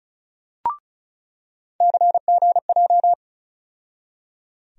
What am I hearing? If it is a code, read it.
Morse code, heard as a keyed tone, about 36 wpm, 700 Hz. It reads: CGJ
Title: USA Call Sign Postfixes - 3 Letters 35wpm